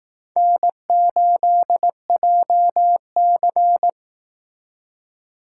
Morse code, heard as a keyed tone, about 18 wpm, 700 Hz. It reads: N8JC
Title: Speed Racing - USA 1x2 Call Signs 15wpm